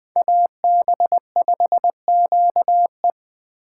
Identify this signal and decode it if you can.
Morse code, heard as a keyed tone, about 20 wpm, 700 Hz. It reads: AB5QE